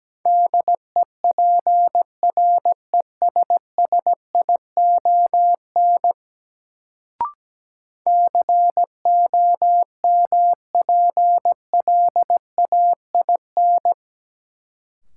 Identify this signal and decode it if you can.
Morse code, heard as a keyed tone, about 17 wpm, 700 Hz. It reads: DEPRESSION COMPLAIN
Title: Sentences from Top 1300 Words - New Word Drill - 17wpm